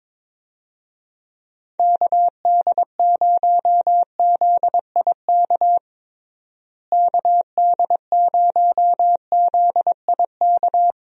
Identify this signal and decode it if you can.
Morse code, heard as a keyed tone, about 22 wpm, 700 Hz. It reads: KD0ZIK KD0ZIK
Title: USA Call Signs - 6 Characters - Repeated 3x - 22wpm